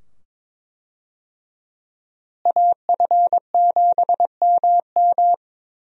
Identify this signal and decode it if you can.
Morse code, heard as a keyed tone, about 21 wpm, 700 Hz. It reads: AF7MM
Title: USA Call Signs - 5 Characters - Repeated 2x - 22wpm